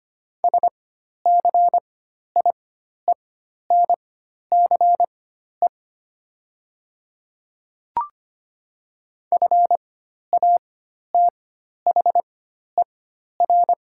Morse code, heard as a keyed tone, about 25 wpm, spaced as for 12 wpm, 700 Hz. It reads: SCIENCE FATHER
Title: 4x Character Spacing - Top 500 Words - 25wpm